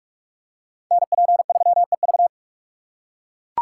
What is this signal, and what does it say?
Morse code, heard as a keyed tone, about 43 wpm, 700 Hz. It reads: NP3EV